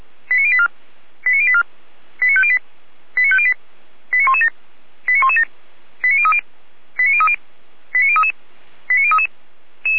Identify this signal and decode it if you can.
Five-tone selective call (ZVEI), heard as quick runs of five tones, tones of about 70 ms. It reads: 89084 89084 89508 89508 89117 89117 89030 89030 89033 89033
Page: ZVEI